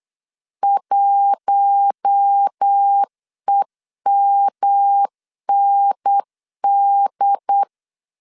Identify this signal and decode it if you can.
Morse code, heard as a keyed tone, about 8 wpm, 800 Hz. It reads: 1EMND
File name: 8wpm